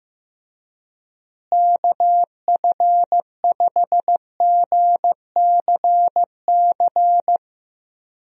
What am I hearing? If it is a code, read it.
Morse code, heard as a keyed tone, about 15 wpm, 700 Hz. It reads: KF5GCC
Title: USA Call Signs - 6 Characters - Repeated 2x - 15wpm